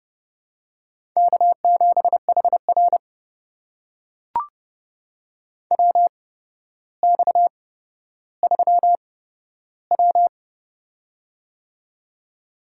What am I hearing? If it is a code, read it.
Morse code, heard as a keyed tone, about 30 wpm, spaced as for 13 wpm, 700 Hz. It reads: K7HR WX3W
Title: USA Call Signs - 4 Characters - 8x Character Spacing - 30wpm